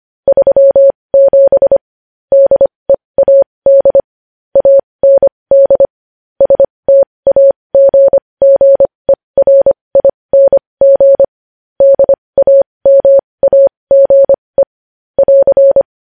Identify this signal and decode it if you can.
Morse code, heard as a keyed tone, about 25 wpm, spaced as for 20 wpm, 550 Hz. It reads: 37 DEAD AND STAGGERING DAMAGE <AR>
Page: News Headlines in Morse Code at 20 WPM